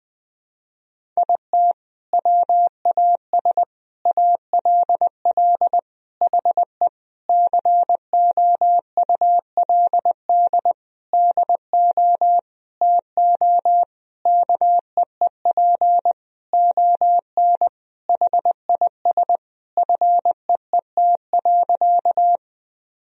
Morse code, heard as a keyed tone, about 20 wpm, 700 Hz. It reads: IT WAS ALL HE COULD DO TO KEEP ON HIS FEET.
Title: Sentences from Top 1000 Words - Review Part 1 - 20wpm